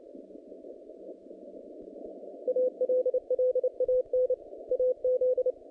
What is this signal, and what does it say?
Morse code, heard as a keyed tone, about 29 wpm, 510 Hz. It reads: ALLAN AZ
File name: contests / naqp-cw-k5tr-op-k5pi- -0000